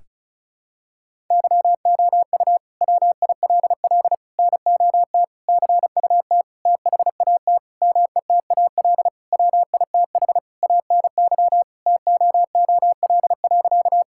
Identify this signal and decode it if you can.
Morse code, heard as a keyed tone, about 34 wpm, 700 Hz. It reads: YOU WILL NOT CUT THAT METAL WITH ANY TOOL.
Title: Sentences from Top 1000 Words - Review Part 6 - 35wpm